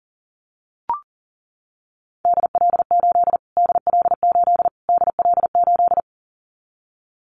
Morse code, heard as a keyed tone, about 40 wpm, 700 Hz. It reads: DL8 DL8 DL8
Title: Callsign Prefixes for Top 25 DXCC - Repeated 3x - 40wpm